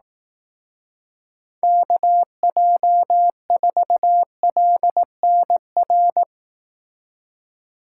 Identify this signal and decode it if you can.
Morse code, heard as a keyed tone, about 18 wpm, 700 Hz. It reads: KJ4LNR
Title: Speed Racing - USA 2x3 Call Signs 15wpm